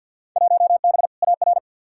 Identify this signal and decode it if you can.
Morse code, heard as a keyed tone, about 50 wpm, 700 Hz. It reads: 1B AR
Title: Field Day Exchanges Encore - 50wpm